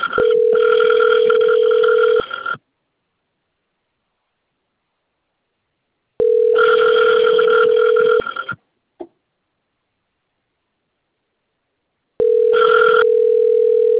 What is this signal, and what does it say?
Ringback tone (North America), 440 + 480 Hz.